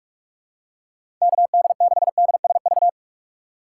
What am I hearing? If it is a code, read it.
Morse code, heard as a keyed tone, about 41 wpm, 700 Hz. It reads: KD6DSV
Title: USA Call Signs - 2x Character Spacing - 45wpm